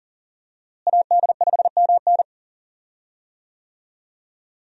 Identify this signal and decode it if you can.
Morse code, heard as a keyed tone, about 40 wpm, 700 Hz. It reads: AD5MN
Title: USA Call Signs - 5 Characters - Repeated 3x - 40wpm